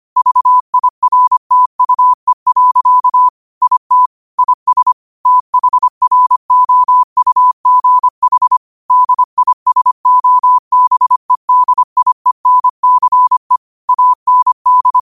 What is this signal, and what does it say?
Morse code, heard as a keyed tone, about 25 wpm, 1000 Hz. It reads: VIRTUE. IT IS THROUGH DISOBEDIENCE AND
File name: qotdfiles / 25wpm